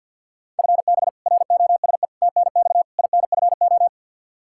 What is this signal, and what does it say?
Morse code, heard as a keyed tone, about 51 wpm, 700 Hz. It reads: FB ROSE TNX INFO
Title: Practice QSO - Exchange 3 - More Info - 50wpm